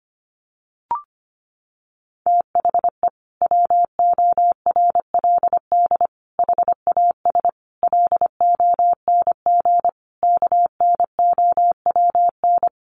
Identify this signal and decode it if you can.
Morse code, heard as a keyed tone, about 25 wpm, 700 Hz. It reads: THE WORLD HAS LONG KNOWN
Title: Sentences from Top 1200 Words 25wpm